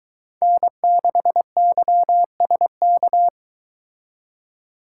Morse code, heard as a keyed tone, about 22 wpm, 700 Hz. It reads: N6YSK